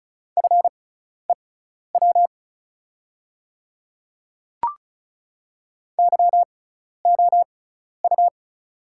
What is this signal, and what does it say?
Morse code, heard as a keyed tone, about 36 wpm, spaced as for 12 wpm, 690 Hz. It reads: FEW YOU
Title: 6x Character Spacing - Top 300 Words - 35wpm